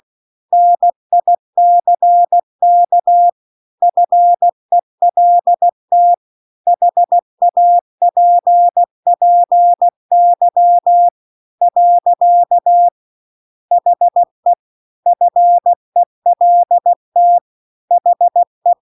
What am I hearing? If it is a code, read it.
Morse code, heard as a keyed tone, about 16 wpm, 700 Hz. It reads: NICK FELT HAPPY . HE FELT HE